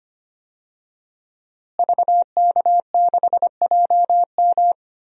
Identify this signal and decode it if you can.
Morse code, heard as a keyed tone, about 25 wpm, 700 Hz. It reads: VK6JM
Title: DXCC-100 Call Signs - 5 Characters - Repeated 2x - 25wpm